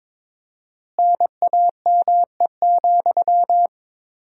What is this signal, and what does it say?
Morse code, heard as a keyed tone, about 21 wpm, 700 Hz. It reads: NAME,
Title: 4x Word-Spacing - Sentences from Top 500 Words - 22wpm